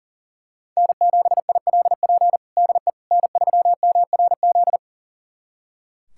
Morse code, heard as a keyed tone, about 40 wpm, 700 Hz. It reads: N7ILP DE N3MRZ
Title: USA Callsign DE Callsign - 5 Characters - 40wpm